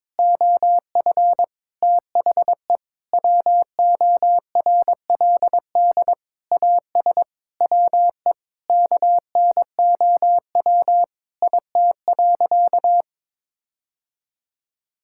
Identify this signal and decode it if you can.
Morse code, heard as a keyed tone, about 22 wpm, 700 Hz. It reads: OF THE WORLD AS WE KNOW IT.
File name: practice / 22wpm / Sentences with 8-Word Phrases - 22wpm